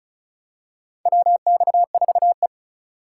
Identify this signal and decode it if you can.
Morse code, heard as a keyed tone, about 34 wpm, 700 Hz. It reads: WX4E
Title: USA Call Signs - 4x Character Spacing - 35wpm